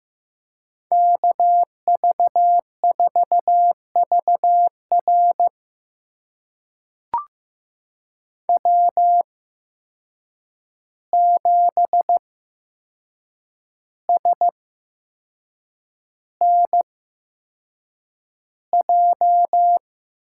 Morse code, heard as a keyed tone, about 15 wpm, spaced as for 6 wpm, 700 Hz. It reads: KV4VR W7SNJ
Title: USA Call Signs - 5 Characters - 8x Character Spacing - 15wpm